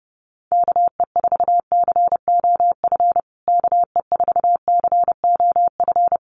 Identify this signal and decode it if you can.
Morse code, heard as a keyed tone, about 30 wpm, 700 Hz. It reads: KE4COF KE4COF
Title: USA Call Signs Repeated 3x 30wpm